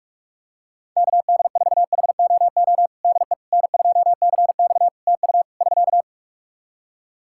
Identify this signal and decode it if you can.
Morse code, heard as a keyed tone, about 48 wpm, 700 Hz. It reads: KD4HOO DE N2CX TU <SK>
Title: Practice QSO - Exchange 4 - Ending - 45wpm